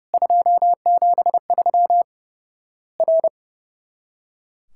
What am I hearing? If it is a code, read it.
Morse code, heard as a keyed tone, about 30 wpm, 700 Hz. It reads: 273 R
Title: Worked All Europe DX Contest - QTC Exchange - 30wpm